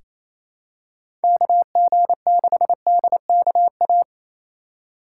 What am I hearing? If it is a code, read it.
Morse code, heard as a keyed tone, about 28 wpm, 700 Hz. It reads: KG6DKA
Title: USA Call Signs - 4x Character Spacing - 28wpm